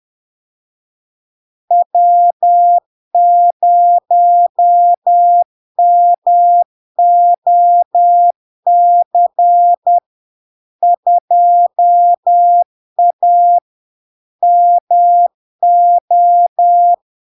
Morse code, heard as a keyed tone, about 10 wpm, 700 Hz. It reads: W0MOC 2A MO
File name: LARK-FD-2022-Log-in-CW-10wpm-700-Hz